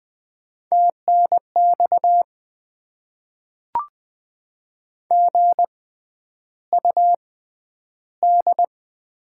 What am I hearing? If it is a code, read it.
Morse code, heard as a keyed tone, about 20 wpm, spaced as for 10 wpm, 690 Hz. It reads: TNX GUD
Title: Top 20 QSO Elements - 6x Character Spacing - 20wpm